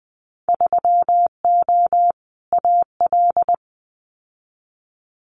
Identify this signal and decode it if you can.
Morse code, heard as a keyed tone, about 20 wpm, 700 Hz. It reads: 3O AL
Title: Winter Field Day Exchanges Encore 20wpm